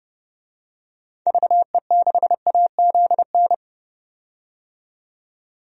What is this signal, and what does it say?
Morse code, heard as a keyed tone, about 32 wpm, 700 Hz. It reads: VE6AZN
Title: DXCC-100 Call Signs - 6 Characters - 30wpm